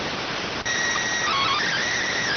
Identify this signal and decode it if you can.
SSTV (slow-scan television), PD 290.